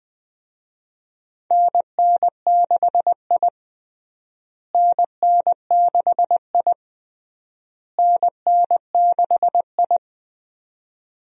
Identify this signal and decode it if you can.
Morse code, heard as a keyed tone, about 19 wpm, 700 Hz. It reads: NN6I NN6I NN6I
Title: USA Call Signs - 4 Characters - Repeated 3x - 20wpm